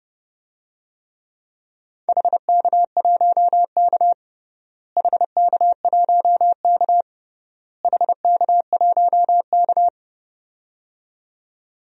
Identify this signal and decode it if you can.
Morse code, heard as a keyed tone, about 31 wpm, 700 Hz. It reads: HK1K HK1K HK1K
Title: DXCC-100 Call Signs - 4 Characters - Repeated 3x - 30wpm